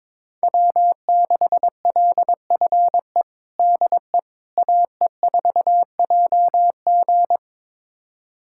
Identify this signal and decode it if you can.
Morse code, heard as a keyed tone, about 22 wpm, 700 Hz. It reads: W6LFE DE AE4JG
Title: USA Callsign DE Callsign - 5 Characters - 22wpm